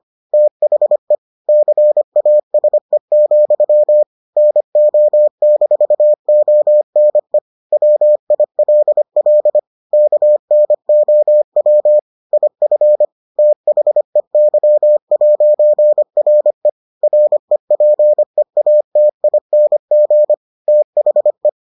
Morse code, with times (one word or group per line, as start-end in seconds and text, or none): 0.3-1.1 THE
1.5-4.0 CASE,
4.4-7.4 NO-ONE
7.7-9.6 WILL
9.9-12.0 KNOW
12.3-13.1 IF
13.4-16.7 THEY'RE
17.0-20.3 REPEATING
20.7-21.5 THE